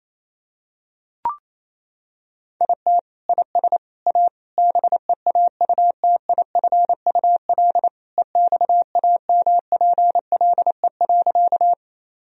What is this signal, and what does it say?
Morse code, heard as a keyed tone, about 28 wpm, 700 Hz. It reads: IT IS A BEAUTIFUL EXAMPLE.
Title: Sentences from Top 1000 Words - Review Part 1 - 28wpm